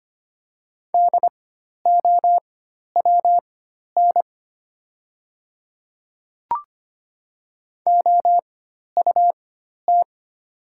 Morse code, heard as a keyed tone, about 25 wpm, spaced as for 12 wpm, 700 Hz. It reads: DOWN OUT
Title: 4x Character Spacing - Top 100 Words - 25wpm